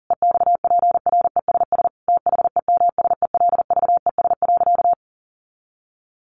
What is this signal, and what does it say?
Morse code, heard as a keyed tone, about 40 wpm, 700 Hz. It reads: EXPRESS THEMSELVES.
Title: Sentences from Top 1100 Words 40wpm